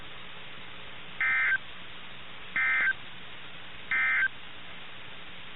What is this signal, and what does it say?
Emergency Alert System (SAME) tones, heard as harsh buzzing bursts. It is heard from 1.1 to 4.4 s.
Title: NOAA Weather Radio - KIH61